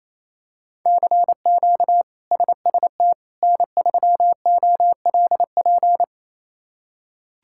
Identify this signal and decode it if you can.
Morse code, heard as a keyed tone, about 28 wpm, 700 Hz. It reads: CQ SST N3OLP
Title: K1USN SST QSO Practice - 28wpm